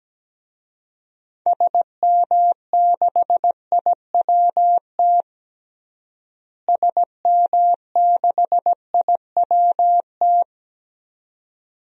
Morse code, heard as a keyed tone, about 17 wpm, 700 Hz. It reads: SM6IWT SM6IWT
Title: DXCC-100 Call Signs - 6 Characters - Repeated 2x - 17wpm